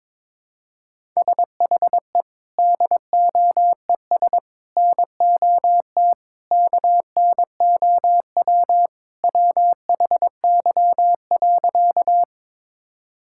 Morse code, heard as a keyed tone, about 22 wpm, 700 Hz. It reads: SHE DOES NOT KNOW WHY.
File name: practice / 22wpm / Sentences with 4-Word Phrases - 22wpm